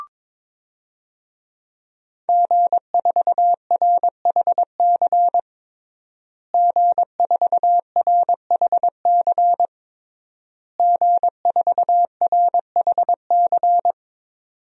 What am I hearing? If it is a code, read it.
Morse code, heard as a keyed tone, about 22 wpm, 700 Hz. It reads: G4RHC G4RHC G4RHC
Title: DXCC-100 Call Signs - 5 Characters - Repeated 3x - 22wpm